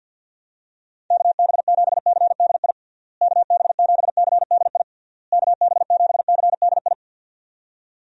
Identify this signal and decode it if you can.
Morse code, heard as a keyed tone, about 52 wpm, 700 Hz. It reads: KB7CDI KB7CDI KB7CDI
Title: USA Call Signs - 6 Characters - Repeated 3x - 50wpm